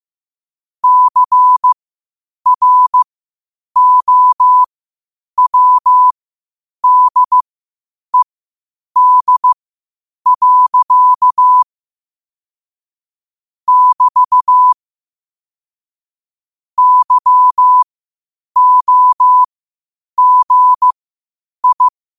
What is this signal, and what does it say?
Morse code, heard as a keyed tone, about 15 wpm, spaced as for 8 wpm, 1000 Hz. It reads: CROWDED. <BT> YOGI
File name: qotdfiles / 8wpm